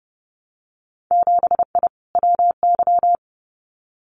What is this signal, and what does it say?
Morse code, heard as a keyed tone, about 30 wpm, 700 Hz. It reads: 7I WY
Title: Winter Field Day Exchanges Encore 30wpm